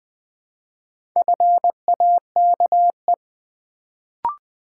Morse code, heard as a keyed tone, about 20 wpm, 700 Hz. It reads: FAKE
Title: Build A Word - 1 Letter At A Time - 4-Letter Words - 20wpm